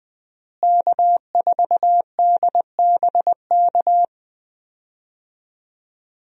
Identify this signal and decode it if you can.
Morse code, heard as a keyed tone, about 20 wpm, 700 Hz. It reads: K4DBK
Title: USA Call Signs - 5 Characters - Repeated 2x - 20wpm